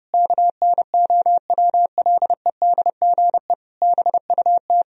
Morse code, heard as a keyed tone, about 30 wpm, 700 Hz. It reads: KNOWLEDGE BUT
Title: Sentences from Top 1000 Words - Review Part 4 - 30wpm